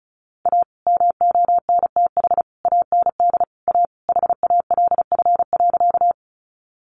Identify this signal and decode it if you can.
Morse code, heard as a keyed tone, about 34 wpm, 700 Hz. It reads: A MONTH AND A HALF.
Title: Sentences from Top 800 Words 35wpm